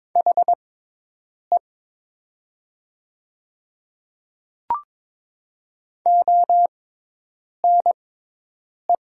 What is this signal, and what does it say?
Morse code, heard as a keyed tone, about 22 wpm, spaced as for 8 wpm, 700 Hz. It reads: HE ONE
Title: 6x Character Spacing - Top 100 Words - 22wpm